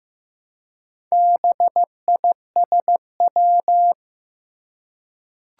Morse code, heard as a keyed tone, about 15 wpm, 700 Hz. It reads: BISW
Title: Four Challenging Letter-Number Combinations 15wpm